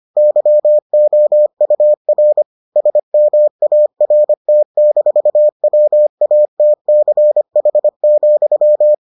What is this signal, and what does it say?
Morse code, heard as a keyed tone, about 25 wpm, 600 Hz. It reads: YOUR SMART-WATCH,